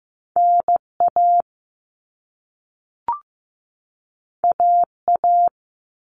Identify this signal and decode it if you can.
Morse code, heard as a keyed tone, about 15 wpm, 700 Hz. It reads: NA AA
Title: All Cut Numbers - 2-Digit 15wpm